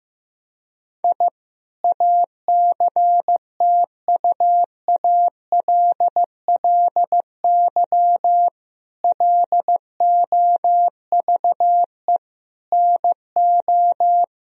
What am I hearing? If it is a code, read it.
Morse code, heard as a keyed tone, about 16 wpm, 700 Hz. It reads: I ACTUALLY LOVE NO
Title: Sentences from Top 1000 Words - Review Part 4 - 15wpm